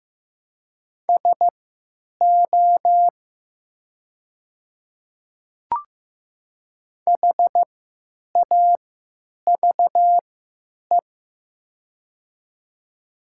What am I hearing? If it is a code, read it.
Morse code, heard as a keyed tone, about 15 wpm, spaced as for 9 wpm, 700 Hz. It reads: SO HAVE
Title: 3x Character Spacing - Top 100 Words - 15wpm